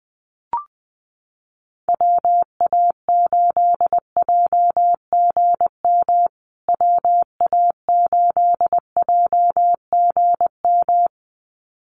Morse code, heard as a keyed tone, about 19 wpm, 700 Hz. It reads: WA8JGM WA8JGM
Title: USA Call Signs Repeated 2x 20wpm